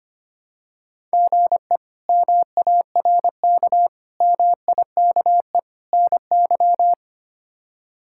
Morse code, heard as a keyed tone, about 25 wpm, 700 Hz. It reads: GE MARK MIKE NY
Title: K1USN SST QSO Practice - 25wpm